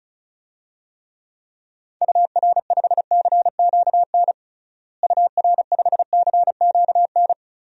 Morse code, heard as a keyed tone, about 36 wpm, 700 Hz. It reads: UR5CQN UR5CQN
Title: DXCC-100 Call Signs - 6 Characters - Repeated 3x - 35wpm